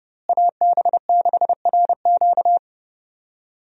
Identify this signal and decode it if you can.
Morse code, heard as a keyed tone, about 30 wpm, 700 Hz. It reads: AB6RQ
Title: USA Call Signs - 5 Characters - Repeated 3x - 30wpm